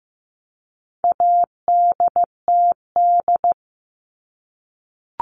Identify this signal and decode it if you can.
Morse code, heard as a keyed tone, about 15 wpm, 700 Hz. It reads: ADTD
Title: All Cut Numbers - 4-Digit 15wpm